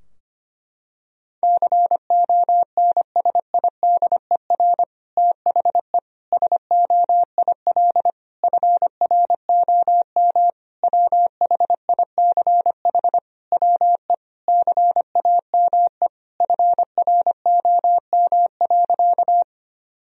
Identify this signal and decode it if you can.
Morse code, heard as a keyed tone, about 25 wpm, 700 Hz. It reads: CONSIDER THE SOIL FROM WHICH WE CAME FROM.
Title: Sentences from Top 1000 Words - Review Part 4 - 25wpm